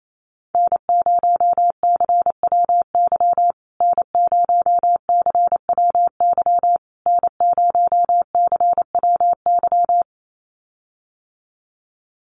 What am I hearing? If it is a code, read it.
Morse code, heard as a keyed tone, about 28 wpm, 700 Hz. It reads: N0CWY N0CWY N0CWY
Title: USA Call Signs Repeated 3x 28wpm